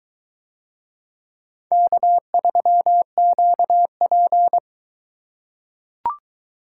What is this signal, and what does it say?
Morse code, heard as a keyed tone, about 22 wpm, 700 Hz. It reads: K3QP